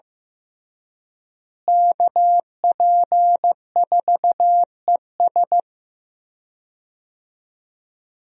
Morse code, heard as a keyed tone, about 15 wpm, 700 Hz. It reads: KP4ES
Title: USA Call Signs - 5 Characters - Repeated 3x - 15wpm